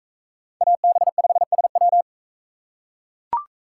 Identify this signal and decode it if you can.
Morse code, heard as a keyed tone, about 40 wpm, 700 Hz. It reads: AB5SW